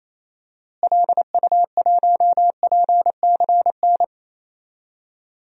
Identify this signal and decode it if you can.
Morse code, heard as a keyed tone, about 28 wpm, 700 Hz. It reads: LU1PCN
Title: DXCC-100 Call Signs - 6 Characters - Repeated 3x - 28wpm